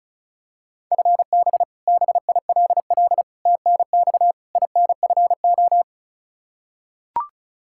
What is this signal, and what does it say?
Morse code, heard as a keyed tone, about 34 wpm, 700 Hz. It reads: FB BILL TNX INFO
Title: Practice QSO - Exchange 3 - More Info - 35wpm